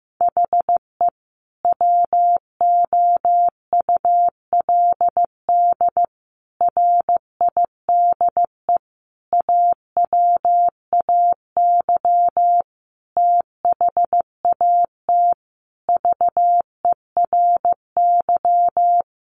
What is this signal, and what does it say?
Morse code, heard as a keyed tone, about 15 wpm, 700 Hz. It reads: HE WOULD RIDE AWAY THAT VERY
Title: Sentences from Top 1200 Words 15wpm